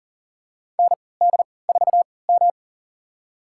Morse code, heard as a keyed tone, about 40 wpm, spaced as for 23 wpm, 700 Hz. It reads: ND4M
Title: USA Call Signs - 4 Characters - 3x Character Spacing - 40wpm